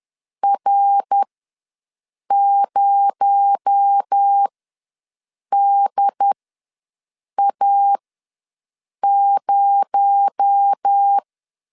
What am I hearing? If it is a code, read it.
Morse code, heard as a keyed tone, about 10 wpm, spaced as for 6 wpm, 800 Hz. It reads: R0DA0
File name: LEZIONI / 6 PKX0 / 6 PKX0 ws 10wpm 4220